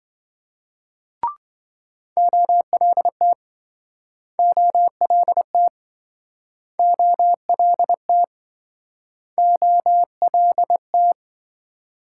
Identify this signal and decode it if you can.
Morse code, heard as a keyed tone, about 27 wpm, 700 Hz. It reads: OLT OLT OLT OLT